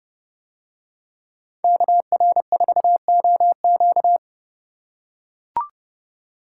Morse code, heard as a keyed tone, about 30 wpm, 700 Hz. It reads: KR4OQ